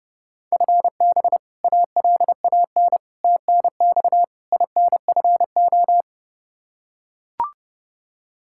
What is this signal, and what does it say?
Morse code, heard as a keyed tone, about 29 wpm, 700 Hz. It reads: FB ALAN TNX INFO
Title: Practice QSO - Exchange 3 - More Info - 30wpm